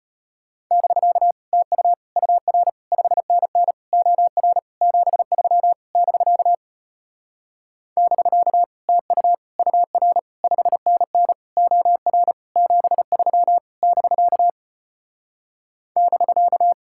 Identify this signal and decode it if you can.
Morse code, heard as a keyed tone, about 33 wpm, 700 Hz. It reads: <BK> TU UR 5NN OR 73 <BK> <BK> TU UR 5NN OR 73 <BK> <BK>